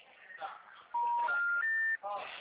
Special information tone, with noise.
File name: out- -2005- - - .36447